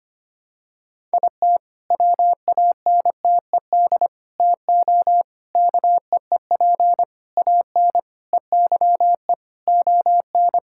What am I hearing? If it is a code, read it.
Morse code, heard as a keyed tone, about 25 wpm, 700 Hz. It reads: IT WANTED TO KEEP AN EYE ON
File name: practice / 25wpm / All-Age Appropriate Jokes - 25wpm